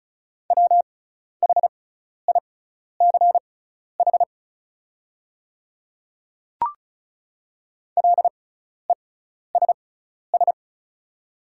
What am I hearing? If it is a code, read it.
Morse code, heard as a keyed tone, about 36 wpm, spaced as for 12 wpm, 690 Hz. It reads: WHICH LESS
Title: 6x Character Spacing - Top 500 Words - 35wpm